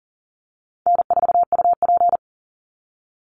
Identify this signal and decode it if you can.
Morse code, heard as a keyed tone, about 40 wpm, 700 Hz. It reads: N4UP
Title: Easy Call Signs Repeated 2x - 40wpm